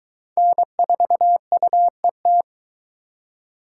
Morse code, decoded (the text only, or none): N4UET